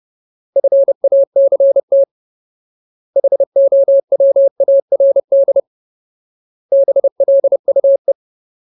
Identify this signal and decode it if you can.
Morse code, heard as a keyed tone, about 30 wpm, 550 Hz. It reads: FACT HOWARD BLUE
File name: Ultimate Word List / 30WPM Ultimate